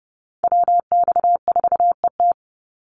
Morse code, heard as a keyed tone, about 30 wpm, 700 Hz. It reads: WX4ET
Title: NA SOTA Chaser Callsigns 30wpm